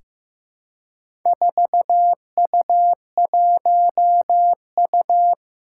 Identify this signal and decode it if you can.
Morse code, heard as a keyed tone, about 15 wpm, 700 Hz. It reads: 4U1U
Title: Four Challenging Letter-Number Combinations 15wpm